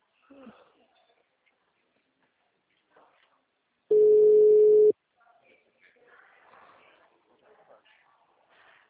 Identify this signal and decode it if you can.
Ringback tone (Europe, 425 Hz), with speech.